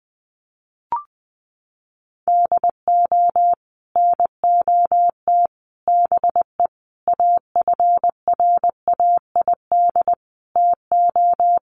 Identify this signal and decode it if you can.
Morse code, heard as a keyed tone, about 20 wpm, 700 Hz. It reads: DO NOT BE AFRAID TO
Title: Sets of 5 Words Encore2 20wpm